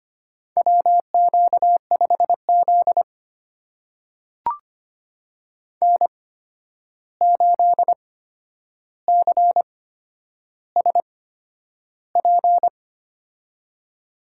Morse code, heard as a keyed tone, about 25 wpm, spaced as for 10 wpm, 700 Hz. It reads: WQ5Z N8CSP
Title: USA Call Signs - 8x Character Spacing - 25wpm